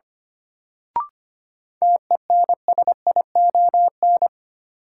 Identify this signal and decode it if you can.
Morse code, heard as a keyed tone, about 25 wpm, 700 Hz. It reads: TENSION
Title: Sentences from Top 1600 Words - New Word Drill - 25wpm